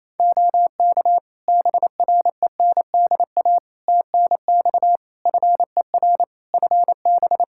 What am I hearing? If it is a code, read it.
Morse code, heard as a keyed tone, about 28 wpm, 700 Hz. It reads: OK BRENDA TNX FER FB
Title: Practice QSO - Exchange 4 - Ending - 28wpm